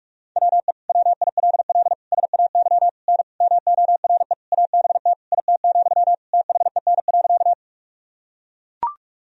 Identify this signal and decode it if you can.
Morse code, heard as a keyed tone, about 41 wpm, 700 Hz. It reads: WE WILL SAY N MORE ABT IT, THEN.
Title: Sentences from Top 200 Words and Abbreviations 45wpm